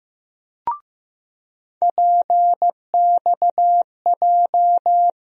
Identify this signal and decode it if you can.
Morse code, heard as a keyed tone, about 15 wpm, 700 Hz. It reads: PXJ
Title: USA Call Sign Postfixes - 3 Letters 15wpm